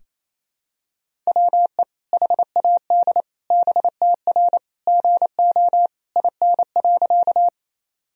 Morse code, heard as a keyed tone, about 27 wpm, 700 Hz. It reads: WE HAD BTR GO IN.
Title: Sentences from Top 400 Words and Abbreviations 28wpm